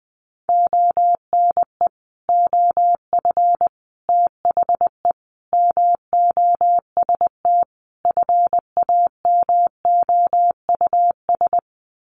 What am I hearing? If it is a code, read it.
Morse code, heard as a keyed tone, about 20 wpm, 700 Hz. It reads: ONE OF THE MOST FAMOUS
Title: Sets of 5 Words Encore2 20wpm